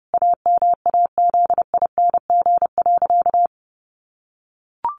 Morse code, heard as a keyed tone, about 30 wpm, 700 Hz. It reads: AMAZING.
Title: Sentences from Top 1500 Words 30wpm